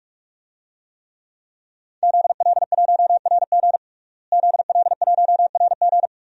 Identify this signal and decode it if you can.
Morse code, heard as a keyed tone, about 46 wpm, 700 Hz. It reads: ZL1RG ZL1RG
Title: DXCC-100 Call Signs - 5 Characters - Repeated 2x - 45wpm